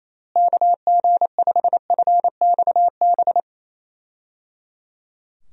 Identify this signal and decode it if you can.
Morse code, heard as a keyed tone, about 27 wpm, 700 Hz. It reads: KG5FXB
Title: USA Call Signs - 6 Characters - Repeated 3x - 28wpm